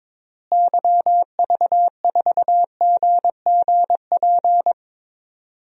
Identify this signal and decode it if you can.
Morse code, heard as a keyed tone, about 22 wpm, 700 Hz. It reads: YV4GGP
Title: DXCC-100 Call Signs - 6 Characters - Repeated 2x - 22wpm